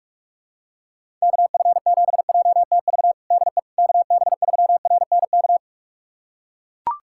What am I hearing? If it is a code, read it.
Morse code, heard as a keyed tone, about 41 wpm, 700 Hz. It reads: KF7JTV DE KB3RNK
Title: USA Callsign DE Callsign 45wpm